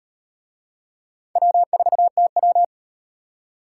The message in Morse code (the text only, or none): W4TW